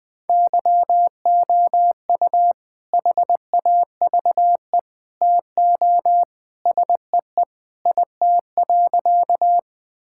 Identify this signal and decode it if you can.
Morse code, heard as a keyed tone, about 20 wpm, 700 Hz. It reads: YOU HAVE TO SEE IT.
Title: Sentences with 2-Word Phrases - Spoken Beginning - 20wpm